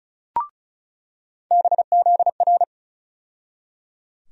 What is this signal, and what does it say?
Morse code, heard as a keyed tone, about 36 wpm, 700 Hz. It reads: BZR
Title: USA Call Sign Postfixes - 3 Letters 35wpm